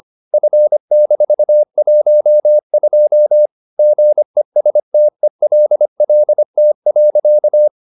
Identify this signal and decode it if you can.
Morse code, heard as a keyed tone, about 25 wpm, 600 Hz. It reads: F-12 GESTELLT.